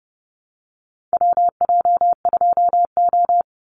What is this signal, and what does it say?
Morse code, heard as a keyed tone, about 30 wpm, 700 Hz. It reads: WJ2O
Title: Call Signs - 30wpm